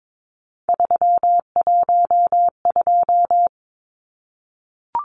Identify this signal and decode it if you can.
Morse code, heard as a keyed tone, about 22 wpm, 700 Hz. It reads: 312
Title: Numbers - 3-Digit 22wpm